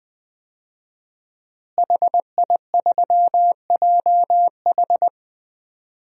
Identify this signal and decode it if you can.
Morse code, heard as a keyed tone, about 20 wpm, 700 Hz. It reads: HI3JH
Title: DXCC-100 Call Signs - 5 Characters - Repeated 2x - 20wpm